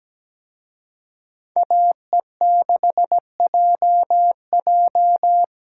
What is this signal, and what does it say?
Morse code, heard as a keyed tone, about 20 wpm, 700 Hz. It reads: AE6JJ
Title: Speed Racing - USA 2x2 Call Signs 17wpm